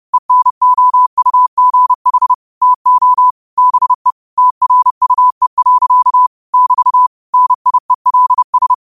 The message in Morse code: ENOUGH TO BE TRUE. <BT> NIELS

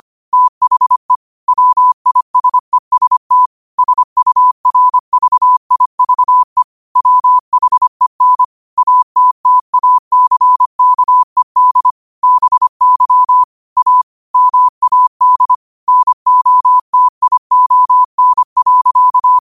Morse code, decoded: THE WISEST SURVIVE WHEN ATTACKED BY A MAD NOTION.

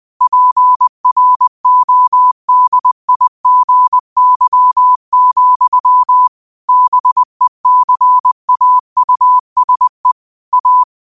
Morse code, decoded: PRODIGY, BECAUSE A

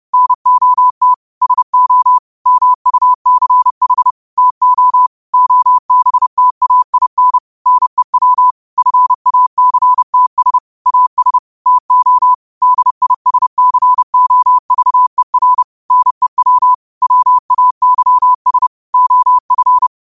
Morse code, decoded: NOT SO MUCH TO OBTAIN NEW FACTS AS TO DISCOVER NEW WAYS OF

